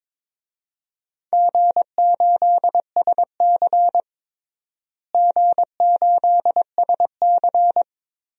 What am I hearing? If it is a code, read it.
Morse code, heard as a keyed tone, about 22 wpm, 700 Hz. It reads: G8SC G8SC
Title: DXCC-100 Call Signs - 4 Characters - Repeated 3x - 22wpm